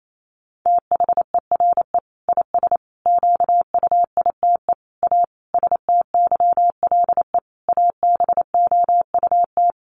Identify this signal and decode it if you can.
Morse code, heard as a keyed tone, about 28 wpm, 700 Hz. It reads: THERE IS QUITE A STYLE ABOUT
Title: Sentences from Top 800 Words 28wpm